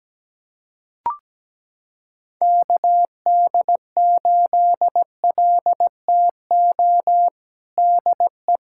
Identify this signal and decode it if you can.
Morse code, heard as a keyed tone, about 17 wpm, 700 Hz. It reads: KD8LTO DE
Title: USA Callsign DE Callsign - 6 Characters - 17wpm